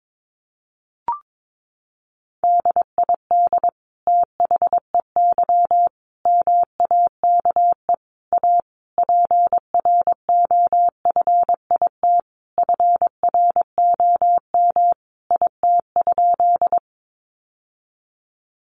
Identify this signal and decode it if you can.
Morse code, heard as a keyed tone, about 22 wpm, 700 Hz. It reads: DID THEY MAKE A PROFIT FROM IT?
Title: Sentences from Top 800 Words 22wpm